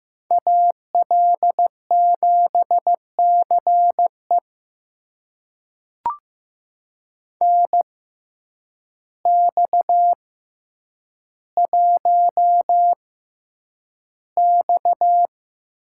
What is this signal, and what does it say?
Morse code, heard as a keyed tone, about 15 wpm, spaced as for 8 wpm, 700 Hz. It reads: AL7CE NX1X
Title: USA Call Signs - 6x Character Spacing - 15wpm